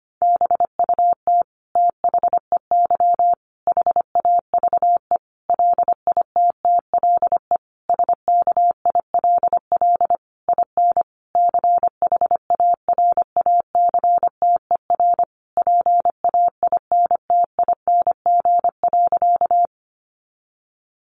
Morse code, heard as a keyed tone, about 25 wpm, 700 Hz. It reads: BUT THEY HAVE LITTLE SKILL IN CHARACTER PAINTING.
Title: Sentences from Top 1200 Words 25wpm